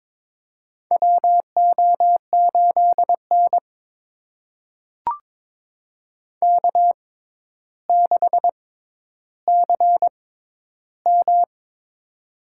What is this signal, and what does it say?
Morse code, heard as a keyed tone, about 22 wpm, spaced as for 11 wpm, 700 Hz. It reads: WO8N K6CM
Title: USA Call Signs - 4 Characters - 6x Character Spacing - 22wpm